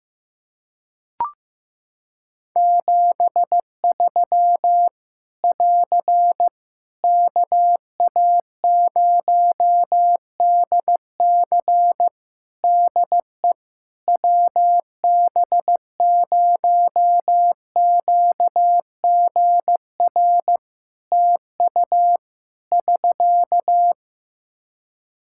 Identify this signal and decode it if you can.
Morse code, heard as a keyed tone, about 15 wpm, 700 Hz. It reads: 73 <AR> KA0DC DE WB0QGR TU <SK>
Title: Practice QSO - Exchange 4 - Ending - 15wpm